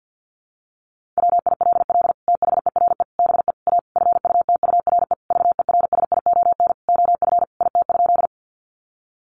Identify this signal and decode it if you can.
Morse code, heard as a keyed tone, about 50 wpm, 700 Hz. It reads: WILL THERE BE A FUTURE VERSION OF IT?
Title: Sentences from Top 1400 Words 50wpm